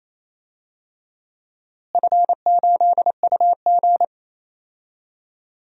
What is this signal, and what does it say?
Morse code, heard as a keyed tone, about 28 wpm, 700 Hz. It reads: F8UG
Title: DXCC-100 Call Signs - 4 Characters - Repeated 2x - 28wpm